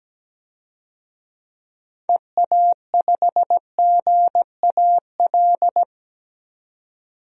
Morse code, heard as a keyed tone, about 17 wpm, 700 Hz. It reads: EA5GAL
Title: DXCC-100 Call Signs - 6 Characters - 17wpm